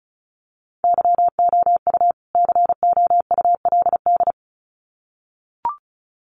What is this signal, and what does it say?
Morse code, heard as a keyed tone, about 34 wpm, 700 Hz. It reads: YOU COULD
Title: Sets of 2 Words 35wpm